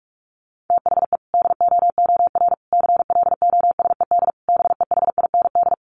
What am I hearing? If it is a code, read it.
Morse code, heard as a keyed tone, about 44 wpm, 700 Hz. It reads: THE DOOR CLOSED BEHIND
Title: Sentences from Top 1400 Words 45wpm